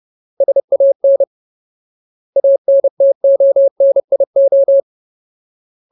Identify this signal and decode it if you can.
Morse code, heard as a keyed tone, about 30 wpm, 550 Hz. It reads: SAN ANTONIO
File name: Ultimate Word List / 30WPM Ultimate